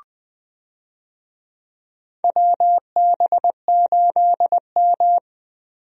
Morse code, heard as a keyed tone, about 20 wpm, 700 Hz. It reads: WB8M
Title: USA Call Signs - 4 Characters - Repeated 3x - 20wpm